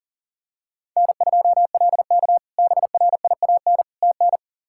Morse code, heard as a keyed tone, about 40 wpm, 700 Hz. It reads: N2LK BRIAN TN
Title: K1USN SST QSO Practice - 40wpm